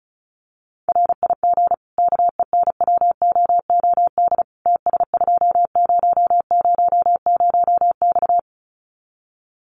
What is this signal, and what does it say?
Morse code, heard as a keyed tone, about 34 wpm, 700 Hz. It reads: RIG KENWOOD TS2000X
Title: Common QSO Phrases 35wpm